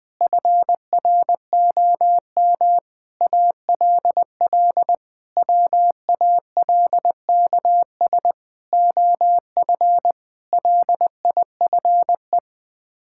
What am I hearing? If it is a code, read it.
Morse code, heard as a keyed tone, about 20 wpm, 700 Hz. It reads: FROM ALL WALKS OF LIFE
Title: Sets of 6 Words - 20wpm